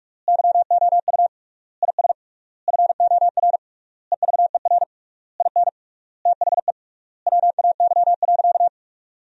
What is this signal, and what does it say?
Morse code, heard as a keyed tone, about 46 wpm, 700 Hz. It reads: YOU IS FOR EVER IN THE WAY.
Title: 3x Word-Spacing - Sentences from Top 400 Words - 45wpm